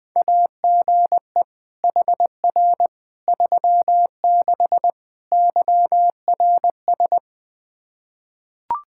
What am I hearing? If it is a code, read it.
Morse code, heard as a keyed tone, about 20 wpm, 700 Hz. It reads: AGE HR 36 YRS
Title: Practice QSO - Exchange 3 - More Info - 20wpm